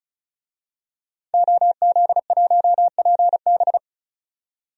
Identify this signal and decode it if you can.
Morse code, heard as a keyed tone, about 34 wpm, 700 Hz. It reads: OZ1PB
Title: DXCC-100 Call Signs - 5 Characters - 35wpm